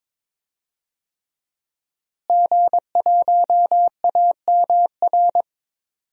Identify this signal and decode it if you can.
Morse code, heard as a keyed tone, about 22 wpm, 700 Hz. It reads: G1AMR
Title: DXCC-100 Call Signs - 5 Characters - Repeated 3x - 22wpm